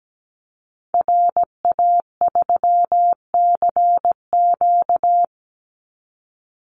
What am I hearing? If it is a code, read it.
Morse code, heard as a keyed tone, about 17 wpm, 700 Hz. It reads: RA3CQ
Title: Call Signs Repeated 3x - 17wpm